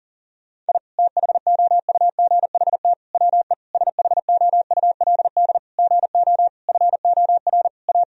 Morse code, heard as a keyed tone, about 40 wpm, 700 Hz. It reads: I THOUGHT WE SHOULD GO FOR A
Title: Sentences from Top 1000 Words - Review Part 2 - 40wpm